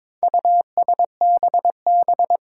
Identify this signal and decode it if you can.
Morse code, heard as a keyed tone, about 22 wpm, 700 Hz. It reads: USBB
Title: Four Challenging Letter-Number Combinations 22wpm